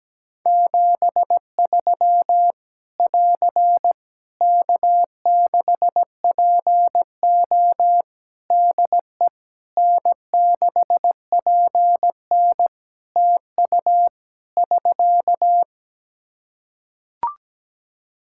Morse code, heard as a keyed tone, about 17 wpm, 700 Hz. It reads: 73 <AR> K6PO DE N6PN TU <SK>
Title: K1USN SST QSO Practice 17wpm